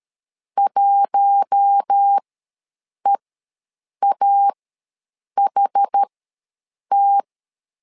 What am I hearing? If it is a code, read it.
Morse code, heard as a keyed tone, about 13 wpm, spaced as for 7 wpm, 800 Hz. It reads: 1EAHT
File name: LEZIONI / 4 Q1 ws 12wpm 2129